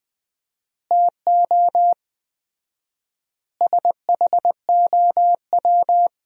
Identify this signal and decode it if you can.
Morse code, heard as a keyed tone, about 21 wpm, 700 Hz. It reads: TO SHOW
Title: 4x Word-Spacing - Sentences from Top 200 Words - 20wpm